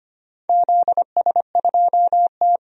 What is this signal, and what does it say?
Morse code, heard as a keyed tone, about 25 wpm, 700 Hz. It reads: ZS2T
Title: DXCC-100 Call Signs - 4 Characters - 25wpm